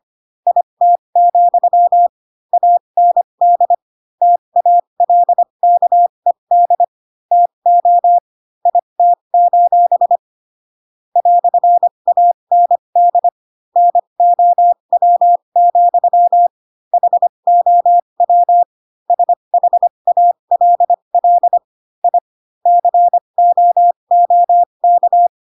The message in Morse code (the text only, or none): IT, AND TALKED TO IT: "AND NOW, HOW SHALL I COOK